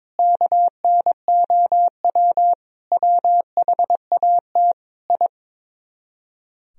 Morse code, heard as a keyed tone, about 23 wpm, 700 Hz. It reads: KNOW WHAT I
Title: Speed-Racing - Sets of 5 Words 22wpm